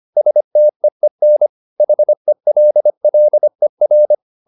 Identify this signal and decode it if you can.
Morse code, heard as a keyed tone, about 25 wpm, 600 Hz. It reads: STEEN HELLER
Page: Ferie 25 wpm